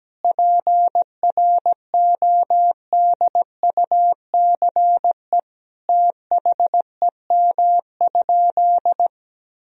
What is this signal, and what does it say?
Morse code, heard as a keyed tone, about 17 wpm, 700 Hz. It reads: PRODUCE THEM?
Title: Sentences from Top 1000 Words - Review Part 3 - 17wpm